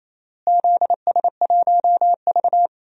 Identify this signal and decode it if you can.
Morse code, heard as a keyed tone, about 28 wpm, 700 Hz. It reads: ZS1V
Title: DXCC-100 Call Signs - 4 Characters - Repeated 3x - 28wpm